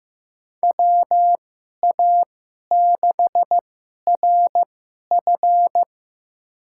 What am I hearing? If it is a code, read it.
Morse code, heard as a keyed tone, about 15 wpm, spaced as for 11 wpm, 700 Hz. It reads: WA6RF
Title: USA Call Signs - 5 Characters - 2x Character Spacing - 15wpm